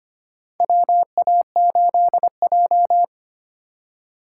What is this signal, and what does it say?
Morse code, heard as a keyed tone, about 25 wpm, 700 Hz. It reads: WA8J
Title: USA Call Signs - 4 Characters - 8x Character Spacing - 25wpm